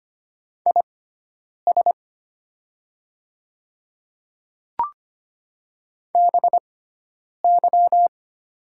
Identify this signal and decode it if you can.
Morse code, heard as a keyed tone, about 25 wpm, spaced as for 9 wpm, 700 Hz. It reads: IS BY
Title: 6x Character Spacing - Top 100 Words - 25wpm